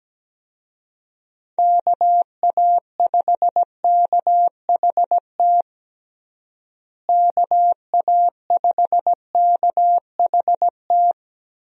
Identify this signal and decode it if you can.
Morse code, heard as a keyed tone, about 17 wpm, 700 Hz. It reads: KA5KHT KA5KHT
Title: USA Call Signs - 6 Characters - Repeated 3x - 17wpm